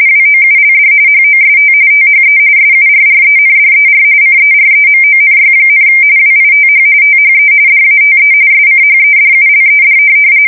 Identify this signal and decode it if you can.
RTTY, about 45.45 baud. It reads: WOULD HOST THE SURRENDER CEREMONY, WITH THE DATE SCHEDULED
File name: fsk170 / USS-Missouri